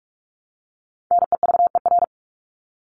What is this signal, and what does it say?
Morse code, heard as a keyed tone, about 44 wpm, 700 Hz. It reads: NEVER
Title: Top 500 Words 45wpm